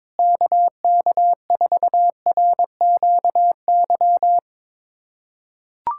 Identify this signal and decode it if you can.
Morse code, heard as a keyed tone, about 22 wpm, 700 Hz. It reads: KK4RQY
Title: USA Callsign DE Callsign 22wpm